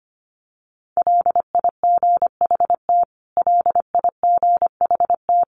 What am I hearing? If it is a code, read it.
Morse code, heard as a keyed tone, about 25 wpm, 700 Hz. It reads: LIGHT LIGHT
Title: Top 500 Words Repeated 25wpm